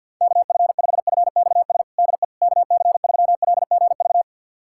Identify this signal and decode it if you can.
Morse code, heard as a keyed tone, about 47 wpm, 700 Hz. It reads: KF5FXS DE KC3LGV
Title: USA Callsign DE Callsign - 6 Characters - 50wpm